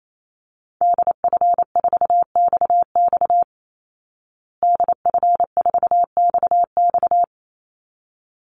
Morse code, heard as a keyed tone, about 28 wpm, 700 Hz. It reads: DF4XX DF4XX
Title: Call Signs Repeated 3x - 28wpm